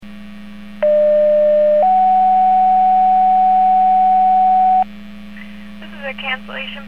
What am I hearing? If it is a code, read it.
Two-tone page, heard as a one-second tone, then a three-second tone, about 600 then 750 Hz. It is heard from 0.8 to 4.8 s.